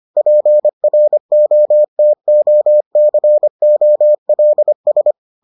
Morse code, heard as a keyed tone, about 25 wpm, 600 Hz. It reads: PROTOCOLS